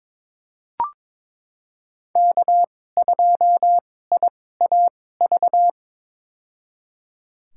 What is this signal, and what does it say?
Morse code, heard as a keyed tone, about 22 wpm, spaced as for 16 wpm, 700 Hz. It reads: K2IAV
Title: USA Call Signs - 5 Characters - 2x Character Spacing - 22wpm